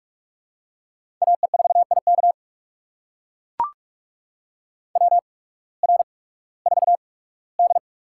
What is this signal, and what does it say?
Morse code, heard as a keyed tone, about 44 wpm, spaced as for 21 wpm, 700 Hz. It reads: AE4IK WR4D
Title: USA Call Signs - 8x Character Spacing - 45wpm